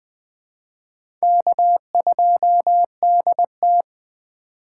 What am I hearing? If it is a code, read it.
Morse code, heard as a keyed tone, about 20 wpm, 700 Hz. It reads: K2DT
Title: USA Call Signs - 4 Characters - 4x Character Spacing - 20wpm